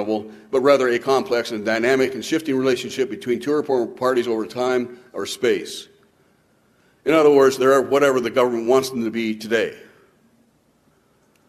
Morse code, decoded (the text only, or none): none